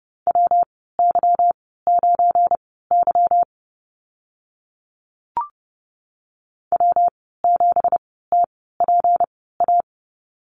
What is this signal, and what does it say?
Morse code, heard as a keyed tone, about 30 wpm, spaced as for 17 wpm, 700 Hz. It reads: WY9Y W7TPA
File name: Lesson 37g - Letter x - Call Signs 30wpm